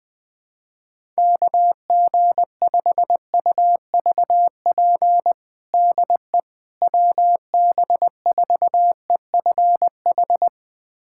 Morse code, heard as a keyed tone, about 20 wpm, 700 Hz. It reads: KG5UVP DE WB4EFH
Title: USA Callsign DE Callsign 20wpm